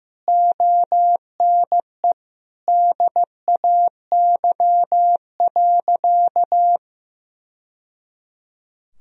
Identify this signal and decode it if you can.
Morse code, heard as a keyed tone, about 16 wpm, 700 Hz. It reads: ONE DAY.
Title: Sentences from Top 1000 Words - Review Part 4 - 15wpm